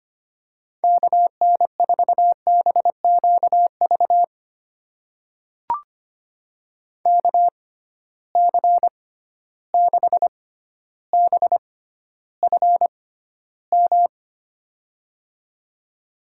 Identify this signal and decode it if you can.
Morse code, heard as a keyed tone, about 25 wpm, spaced as for 13 wpm, 700 Hz. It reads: KN4BQV KC6BFM
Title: USA Call Signs - 6 Characters - 6x Character Spacing - 25wpm